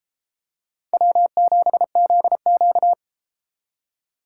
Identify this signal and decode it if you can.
Morse code, heard as a keyed tone, about 31 wpm, 700 Hz. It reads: W7ZQ